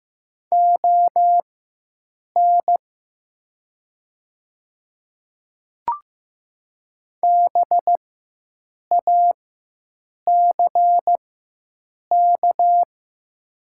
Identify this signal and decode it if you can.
Morse code, heard as a keyed tone, about 15 wpm, spaced as for 7 wpm, 700 Hz. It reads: ON BACK
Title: 4x Character Spacing - Top 300 Words - 15wpm